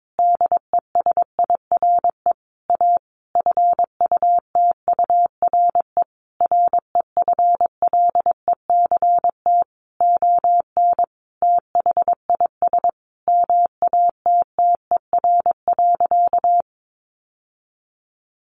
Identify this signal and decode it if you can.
Morse code, heard as a keyed tone, about 22 wpm, 700 Hz. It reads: DESIRE A FUTURE REFLECT ON THIS MATTER.
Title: Sentences from Top 1500 Words 22wpm